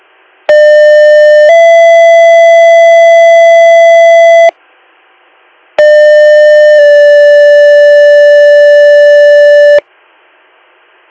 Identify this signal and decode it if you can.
Two-tone page, heard as a one-second tone, then a three-second tone, about 600 then 670 Hz, about 600 then 580 Hz. It is heard from 0.5 to 9.8 s.